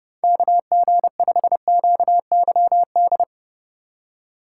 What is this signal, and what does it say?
Morse code, heard as a keyed tone, about 30 wpm, 700 Hz. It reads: KG5QYD
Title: USA Call Signs - 4x Character Spacing - 30wpm